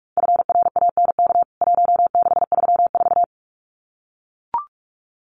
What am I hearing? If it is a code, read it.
Morse code, heard as a keyed tone, about 44 wpm, 700 Hz. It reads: FRANK 1634
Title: CWT Exchange 45wpm